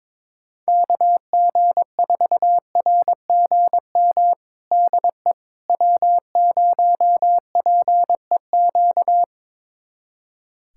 Morse code, heard as a keyed tone, about 22 wpm, 700 Hz. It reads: KG4RGM DE W0PEQ
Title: USA Callsign DE Callsign 22wpm